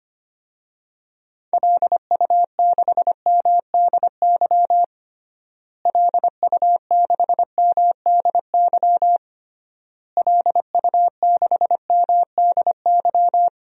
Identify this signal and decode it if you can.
Morse code, heard as a keyed tone, about 25 wpm, 700 Hz. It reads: LU6MDY LU6MDY LU6MDY
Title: DXCC-100 Call Signs - 6 Characters - Repeated 3x - 25wpm